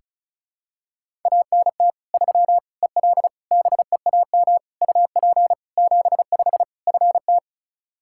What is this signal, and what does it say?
Morse code, heard as a keyed tone, about 34 wpm, 700 Hz. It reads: ANT 3 EL BEAM UP 75 FT
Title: Practice QSO - Exchange 2 - Info - 35wpm